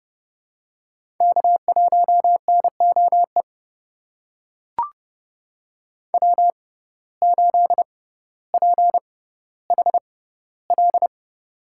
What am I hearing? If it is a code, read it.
Morse code, heard as a keyed tone, about 32 wpm, spaced as for 16 wpm, 700 Hz. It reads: K1NOE W8PHL
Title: USA Call Signs - 5 Characters - 6x Character Spacing - 30wpm